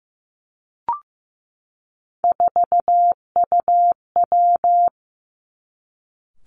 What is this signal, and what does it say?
Morse code, heard as a keyed tone, about 15 wpm, 700 Hz. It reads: 4UW
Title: Three Challenging Letter-Number Combinations 15wpm